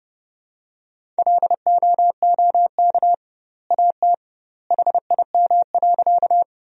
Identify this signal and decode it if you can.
Morse code, heard as a keyed tone, about 30 wpm, 700 Hz. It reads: LOOK AT HIM.
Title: 2x Word-Spacing - Sentences from Top 100 Words - 30wpm